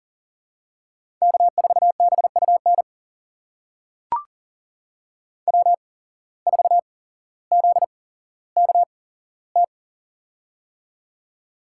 Morse code, heard as a keyed tone, about 40 wpm, spaced as for 17 wpm, 700 Hz. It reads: K4BUN W4ZKT
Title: USA Call Signs - 5 Characters - 8x Character Spacing - 40wpm